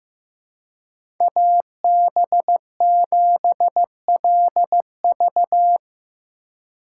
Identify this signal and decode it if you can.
Morse code, heard as a keyed tone, about 15 wpm, 700 Hz. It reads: AB7LV
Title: USA Call Signs - 5 Characters - 2x Character Spacing - 15wpm